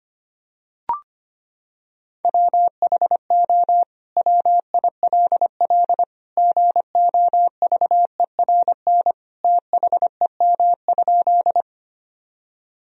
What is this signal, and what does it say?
Morse code, heard as a keyed tone, about 25 wpm, 700 Hz. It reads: WHO WILL GOVERN THEM?
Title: Sentences from Top 1000 Words - Review Part 2 - 25wpm